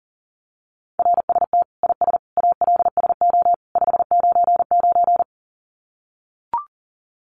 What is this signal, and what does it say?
Morse code, heard as a keyed tone, about 40 wpm, 700 Hz. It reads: RST IS ALSO 599
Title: Common QSO Phrases 40wpm